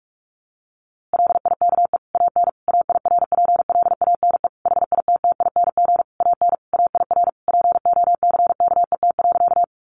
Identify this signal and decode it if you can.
Morse code, heard as a keyed tone, about 44 wpm, 700 Hz. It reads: LIKE AN AIRPLANE HITTING AN AIR POCKET.
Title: A Sound Of Thunder 45wpm